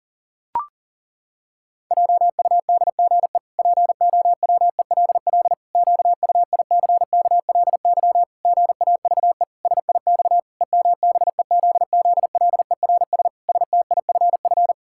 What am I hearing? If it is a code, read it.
Morse code, heard as a keyed tone, about 40 wpm, 700 Hz. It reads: JUDGE POWELL QUICKLY GAVE SIX EMBEZZLERS STIFF
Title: Pangram Challenge v2 - 40wpm